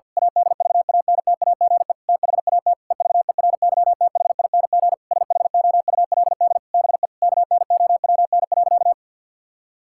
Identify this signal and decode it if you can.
Morse code, heard as a keyed tone, about 50 wpm, 700 Hz. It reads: ADVANTAGE THAT EVERYTHING SHOULD BE KNOWN.
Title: Sentences from Top 1000 Words - Review Part 6 - 50wpm